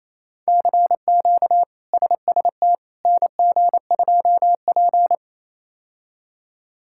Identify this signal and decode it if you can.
Morse code, heard as a keyed tone, about 28 wpm, 700 Hz. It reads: CQ SST NG2P
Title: K1USN SST QSO Practice - 28wpm